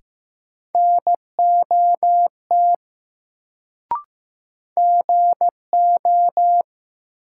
Morse code, heard as a keyed tone, about 15 wpm, 700 Hz. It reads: NOT GO
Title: Sets of 2 Words - New Word Drill - 15wpm